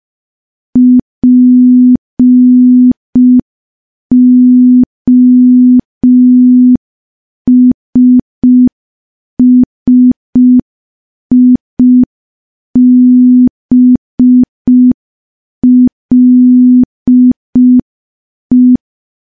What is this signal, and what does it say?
Morse code, heard as a keyed tone, about 5 wpm, 250 Hz. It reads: POSSIBLE
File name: morse / morse-training-05WPM